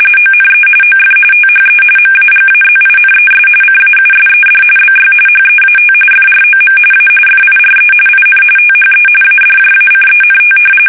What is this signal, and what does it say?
RTTY, about 45.45 baud. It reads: ONE OF THE MAGAZINES CAUSED BY HEAT FROM A FIRE IN THE ADJACENT
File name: fsk850 / usnavy3